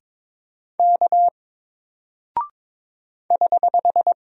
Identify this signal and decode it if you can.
Morse code, heard as a keyed tone, about 22 wpm, 700 Hz. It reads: K <HH>
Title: Single Character (Letter:Number:Punctuation:Prosign) Alt 22wpm